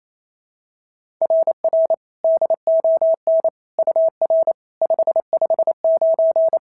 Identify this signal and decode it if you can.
Morse code, heard as a keyed tone, about 28 wpm, 690 Hz. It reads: RR DON UR 559
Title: SOTA QSO Practice 28wpm